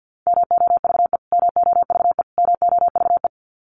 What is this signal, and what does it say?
Morse code, heard as a keyed tone, about 50 wpm, 700 Hz. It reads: MOVE MOVE MOVE
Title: Top 300 Words Repeated 50wpm